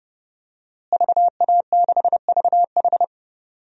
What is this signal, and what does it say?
Morse code, heard as a keyed tone, about 30 wpm, 700 Hz. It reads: VA6VH
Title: DXCC-100 Call Signs - 5 Characters - Repeated 3x - 30wpm